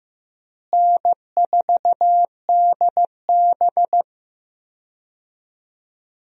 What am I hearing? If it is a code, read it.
Morse code, heard as a keyed tone, about 15 wpm, 700 Hz. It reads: N4DB
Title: USA Call Signs - 4 Characters - Repeated 2x - 15wpm